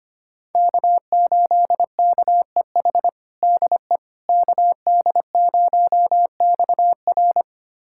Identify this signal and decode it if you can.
Morse code, heard as a keyed tone, about 25 wpm, 700 Hz. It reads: K8KEH DE KD0XR
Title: USA Callsign DE Callsign - 5 Characters - 25wpm